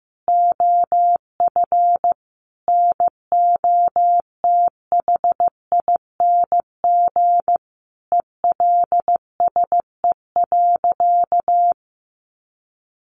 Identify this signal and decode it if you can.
Morse code, heard as a keyed tone, about 14 wpm, 700 Hz. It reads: OF NOTHING ELSE.
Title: Sentences from Top 800 Words 15wpm